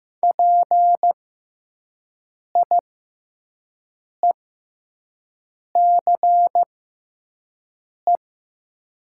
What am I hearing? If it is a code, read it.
Morse code, heard as a keyed tone, about 15 wpm, spaced as for 5 wpm, 700 Hz. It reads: PIECE
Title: 6x Character Spacing - Top 500 Words - 15wpm